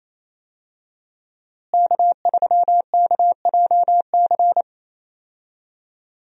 Morse code, heard as a keyed tone, about 28 wpm, 700 Hz. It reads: K3KJC
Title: USA Call Signs - 5 Characters - Repeated 2x - 28wpm